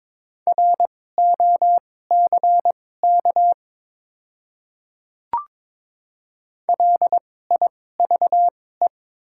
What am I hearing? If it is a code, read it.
Morse code, heard as a keyed tone, about 22 wpm, spaced as for 16 wpm, 700 Hz. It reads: ROCK LIVE
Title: 2x Character Spacing - Top 500 Words - 22wpm